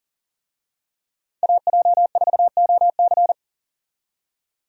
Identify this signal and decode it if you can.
Morse code, heard as a keyed tone, about 40 wpm, 700 Hz. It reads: AJ4OC
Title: USA Call Signs - 5 Characters - 40wpm